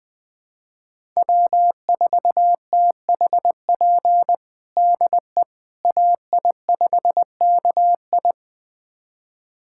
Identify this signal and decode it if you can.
Morse code, heard as a keyed tone, about 20 wpm, 700 Hz. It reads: W4THP DE AI5KI
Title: USA Callsign DE Callsign - 5 Characters - 20wpm